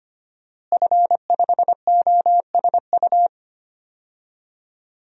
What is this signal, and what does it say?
Morse code, heard as a keyed tone, about 25 wpm, 700 Hz. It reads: F5OSU
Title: DXCC-100 Call Signs - 5 Characters - Repeated 2x - 25wpm